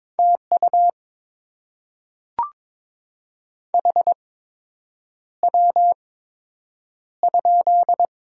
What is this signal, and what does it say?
Morse code, heard as a keyed tone, about 22 wpm, spaced as for 8 wpm, 700 Hz. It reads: TU HW?
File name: Top 20 QSO Elements - 8x Character Spacing - 22wpm